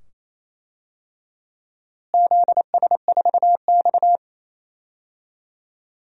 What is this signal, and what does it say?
Morse code, heard as a keyed tone, about 28 wpm, 700 Hz. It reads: ZS4X
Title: DXCC-100 Call Signs - 4 Characters - Repeated 2x - 28wpm